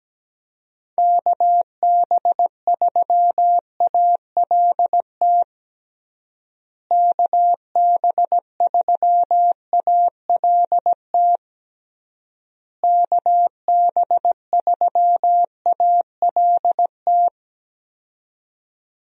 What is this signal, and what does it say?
Morse code, heard as a keyed tone, about 17 wpm, 700 Hz. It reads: KB3ALT KB3ALT KB3ALT
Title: USA Call Signs - 6 Characters - Repeated 3x - 17wpm